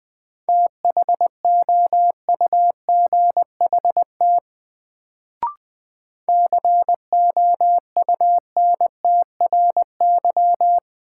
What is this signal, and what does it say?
Morse code, heard as a keyed tone, about 20 wpm, 700 Hz. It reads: THOUGHT COUNTRY
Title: Sets of 2 Words - New Word Drill - 20wpm